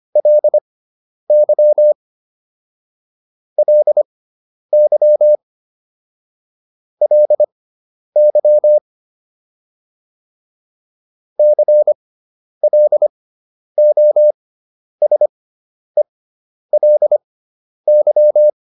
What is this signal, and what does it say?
Morse code, heard as a keyed tone, about 25 wpm, spaced as for 10 wpm, 600 Hz. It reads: LY LY LY CLOSELY